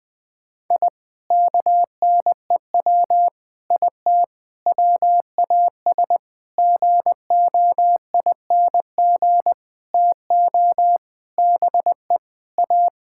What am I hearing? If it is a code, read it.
Morse code, heard as a keyed tone, about 20 wpm, 700 Hz. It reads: I KNEW IT WAS GOING TO BE A
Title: Sentences with 5-Word Phrases - 20wpm